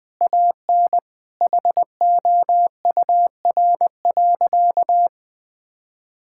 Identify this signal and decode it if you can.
Morse code, heard as a keyed tone, about 20 wpm, 700 Hz. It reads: AN HOUR.
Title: Sentences with 3-Word Phrases - Spoken Beginning - 20wpm